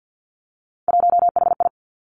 Morse code, heard as a keyed tone, about 50 wpm, 700 Hz. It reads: 1HI
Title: Three Challenging Letter-Number Combinations 50wpm